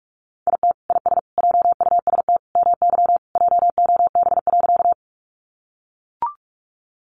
Morse code, heard as a keyed tone, about 44 wpm, 700 Hz. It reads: IT IS JUST MY JOB.
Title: Sentences from Top 600 Words 45wpm